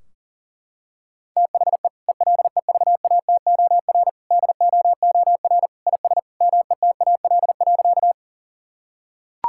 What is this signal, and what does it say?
Morse code, heard as a keyed tone, about 40 wpm, 700 Hz. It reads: THE ELEVATOR DOOR IS METAL.
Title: Sentences from Top 1600 Words 40wpm